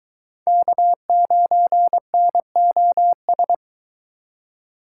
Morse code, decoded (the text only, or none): K9NOS